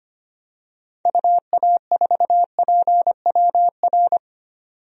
Morse code, heard as a keyed tone, about 25 wpm, 700 Hz. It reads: UA4PWR
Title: DXCC-100 Call Signs - 6 Characters - 25wpm